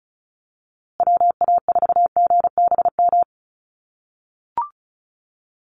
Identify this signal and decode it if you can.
Morse code, heard as a keyed tone, about 34 wpm, 700 Hz. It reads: WA4GBM
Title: USA Call Signs Repeated 3x 35wpm